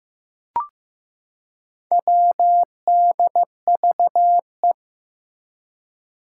Morse code, heard as a keyed tone, about 15 wpm, 700 Hz. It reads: WDVE
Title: Four Challenging Letter-Number Combinations 15wpm